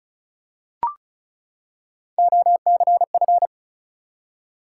Morse code, heard as a keyed tone, about 36 wpm, 700 Hz. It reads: OCF
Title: Top 180 QSO Elements 35wpm